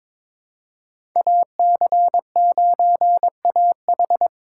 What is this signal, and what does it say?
Morse code, heard as a keyed tone, about 22 wpm, 700 Hz. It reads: AC9AH
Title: USA Call Signs - 5 Characters - 6x Character Spacing - 22wpm